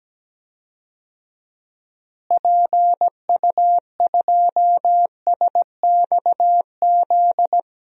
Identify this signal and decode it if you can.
Morse code, heard as a keyed tone, about 17 wpm, 700 Hz. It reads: PU2SXZ
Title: DXCC-100 Call Signs - 6 Characters - Repeated 2x - 17wpm